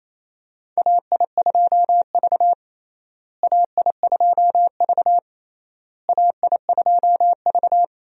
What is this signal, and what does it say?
Morse code, heard as a keyed tone, about 28 wpm, 700 Hz. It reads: AI2V AI2V AI2V
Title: USA Call Signs - 4 Characters - Repeated 3x - 28wpm